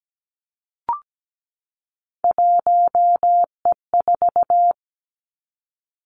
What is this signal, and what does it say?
Morse code, heard as a keyed tone, about 17 wpm, 700 Hz. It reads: 1E4
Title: Three Challenging Letter-Number Combinations 17wpm